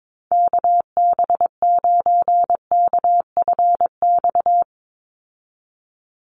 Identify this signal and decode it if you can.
Morse code, heard as a keyed tone, about 22 wpm, 700 Hz. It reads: KB9KFX
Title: USA Call Signs 22wpm